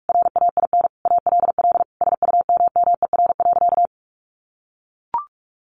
Morse code, heard as a keyed tone, about 44 wpm, 700 Hz. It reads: RAIN ALL SUMMER.
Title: Sentences from Top 1100 Words 45wpm